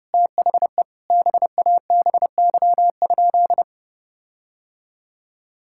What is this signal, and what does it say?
Morse code, heard as a keyed tone, about 30 wpm, 700 Hz. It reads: THE BABY?
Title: Sentences from Top 1600 Words 30wpm